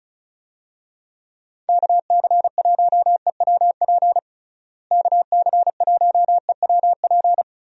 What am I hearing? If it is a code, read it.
Morse code, heard as a keyed tone, about 34 wpm, 700 Hz. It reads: KC1EWP KC1EWP
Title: USA Call Signs - 6 Characters - Repeated 2x - 35wpm